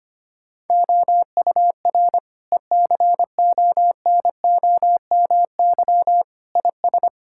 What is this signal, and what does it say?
Morse code, heard as a keyed tone, about 25 wpm, 700 Hz. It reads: OUR ECONOMY IS
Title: Sentences from Top 1000 Words - Review Part 6 - 25wpm